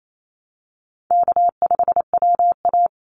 Morse code, heard as a keyed tone, about 28 wpm, 700 Hz. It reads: K5WA
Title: Call Signs Repeated 2x - 28wpm